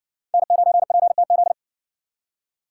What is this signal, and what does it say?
Morse code, heard as a keyed tone, about 54 wpm, 700 Hz. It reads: N9PTZ